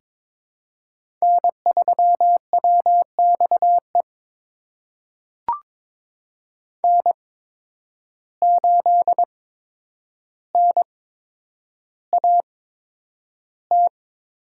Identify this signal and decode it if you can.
Morse code, heard as a keyed tone, about 22 wpm, spaced as for 9 wpm, 700 Hz. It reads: N3WXE N8NAT
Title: USA Call Signs - 8x Character Spacing - 22wpm